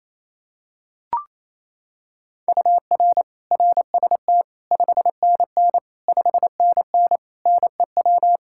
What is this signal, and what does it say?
Morse code, heard as a keyed tone, about 28 wpm, 700 Hz. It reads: UR RST 5NN 5NN NEW
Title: Practice QSO - Exchange 1 - Beginning - 28wpm